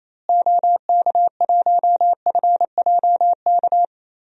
Morse code, heard as a keyed tone, about 28 wpm, 700 Hz. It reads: OK1FJK
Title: DXCC-100 Call Signs - 6 Characters - Repeated 3x - 28wpm